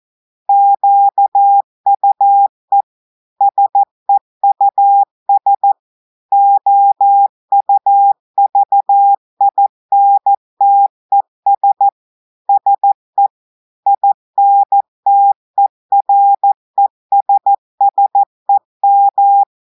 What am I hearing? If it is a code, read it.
Morse code, heard as a keyed tone, about 14 wpm, 800 Hz. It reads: QUE SEUS OUVINTES SE INTERESSEM
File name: audios / portugais / portugais texte en clair 2 vitesse 14 mots minutes